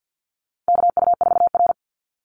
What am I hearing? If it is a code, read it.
Morse code, heard as a keyed tone, about 50 wpm, 700 Hz. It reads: KU4R
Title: NA SOTA Chaser Callsigns 50wpm